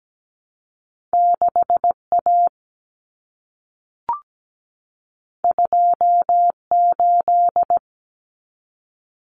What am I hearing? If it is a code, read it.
Morse code, heard as a keyed tone, about 17 wpm, 700 Hz. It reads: 6A 28
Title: Common Cut Numbers - 2-Digit 17wpm